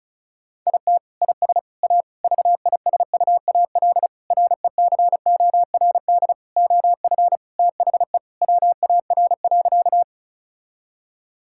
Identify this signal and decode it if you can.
Morse code, heard as a keyed tone, about 36 wpm, 700 Hz. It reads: IT IS A VISUAL RECORD OF THE WAR.
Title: Sentences from Top 1000 Words - Review Part 6 - 35wpm